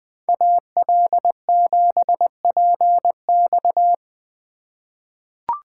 Morse code, heard as a keyed tone, about 20 wpm, 700 Hz. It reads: AL7PX
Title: USA Call Signs - 5 Characters - 4x Character Spacing - 20wpm